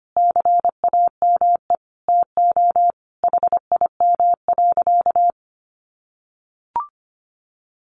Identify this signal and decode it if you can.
Morse code, heard as a keyed tone, about 25 wpm, 700 Hz. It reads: CAME TO HIM.
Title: Sentences from Top 1000 Words 25wpm